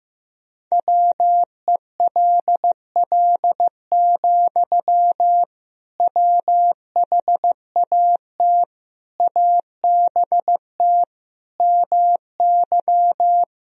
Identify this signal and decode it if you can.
Morse code, heard as a keyed tone, about 15 wpm, 700 Hz. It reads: WELL, WHAT ABT MY
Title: Sentences from Top 500 Words and Abbreviations 15wpm